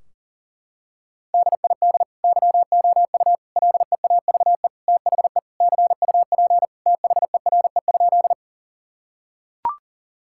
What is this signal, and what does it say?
Morse code, heard as a keyed tone, about 40 wpm, 700 Hz. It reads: DID YOU LEAVE THE CUP THERE?
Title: Sentences from Top 1600 Words 40wpm